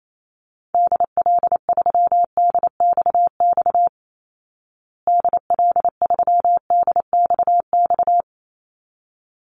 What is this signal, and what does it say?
Morse code, heard as a keyed tone, about 28 wpm, 700 Hz. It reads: DL3DXX DL3DXX
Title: Call Signs Repeated 3x - 28wpm